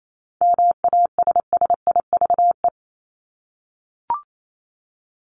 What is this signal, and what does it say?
Morse code, heard as a keyed tone, about 28 wpm, 700 Hz. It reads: MASSIVE
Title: Top 1751-2000 Words 28wpm